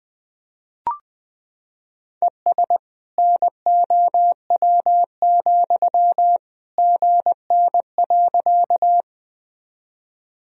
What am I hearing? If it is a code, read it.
Morse code, heard as a keyed tone, about 20 wpm, 700 Hz. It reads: ES NOW, GN.
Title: Sentences from Top 300 Words and Abbreviations 20wpm